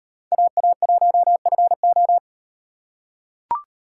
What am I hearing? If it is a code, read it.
Morse code, heard as a keyed tone, about 38 wpm, 700 Hz. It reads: AA1FO